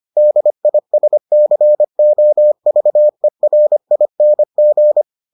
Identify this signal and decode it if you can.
Morse code, heard as a keyed tone, about 25 wpm, 600 Hz. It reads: DISCOVERING